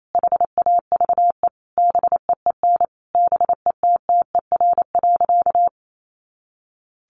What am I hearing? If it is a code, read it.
Morse code, heard as a keyed tone, about 28 wpm, 700 Hz. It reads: HAVE BEEN BETTER.
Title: Sentences from Top 500 Words 28wpm